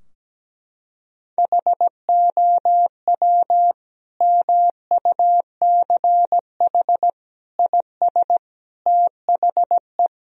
Morse code, heard as a keyed tone, about 17 wpm, 700 Hz. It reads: HOW MUCH IS THE
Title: Sentences from Top 1000 Words - Review Part 4 - 17wpm